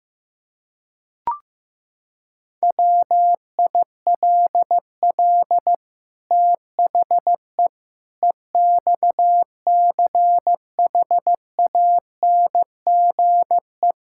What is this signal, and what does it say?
Morse code, heard as a keyed tone, about 16 wpm, 700 Hz. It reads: WILL THE EXCHANGE
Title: Sentences from Top 1000 Words - Review Part 6 - 15wpm